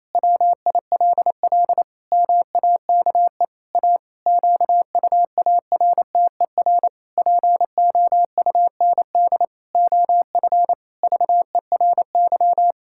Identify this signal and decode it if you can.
Morse code, heard as a keyed tone, about 28 wpm, 700 Hz. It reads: WILL MAKE A QUARTER POUND OF VERY
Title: Pangram Challenge v2 - 28wpm